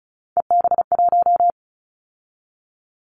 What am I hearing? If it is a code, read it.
Morse code, heard as a keyed tone, about 36 wpm, 700 Hz. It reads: EB1
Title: Callsign Prefixes for Top 25 DXCC - 35wpm